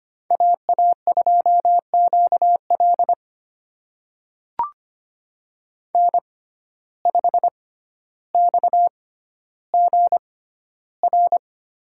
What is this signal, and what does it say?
Morse code, heard as a keyed tone, about 25 wpm, spaced as for 13 wpm, 700 Hz. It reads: AA2QL N5XGR
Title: USA Call Signs - 5 Characters - 6x Character Spacing - 25wpm